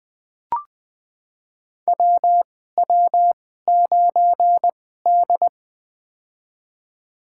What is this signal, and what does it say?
Morse code, heard as a keyed tone, about 20 wpm, spaced as for 15 wpm, 700 Hz. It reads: WW9D
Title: USA Call Signs - 4 Characters - 2x Character Spacing - 20wpm